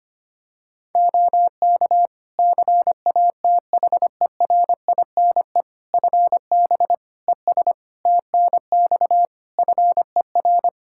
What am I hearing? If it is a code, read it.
Morse code, heard as a keyed tone, about 25 wpm, 700 Hz. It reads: OK CATHERINE FB ES TNX FER
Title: Practice QSO - Exchange 2 - Info - 25wpm